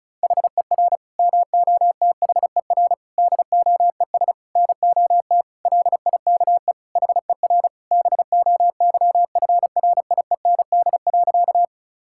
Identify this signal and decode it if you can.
Morse code, heard as a keyed tone, about 36 wpm, 700 Hz. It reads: HER MOTHER DOES NOT LIKE HER BOYFRIEND.
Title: Sentences from Top 1000 Words - Review Part 5 - 35wpm